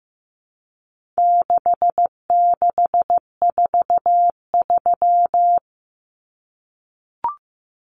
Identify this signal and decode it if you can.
Morse code, heard as a keyed tone, about 15 wpm, 700 Hz. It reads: 6643
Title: Common Cut Numbers - 4-Digit 15wpm